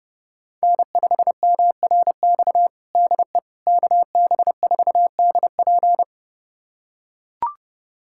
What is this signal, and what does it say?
Morse code, heard as a keyed tone, about 30 wpm, 700 Hz. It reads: N5MRX DE KB4DP
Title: USA Callsign DE Callsign - 5 Characters - 30wpm